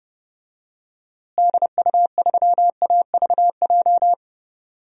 Morse code, heard as a keyed tone, about 30 wpm, 700 Hz. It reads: DU3AVJ
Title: DXCC-100 Call Signs - 6 Characters - Repeated 2x - 30wpm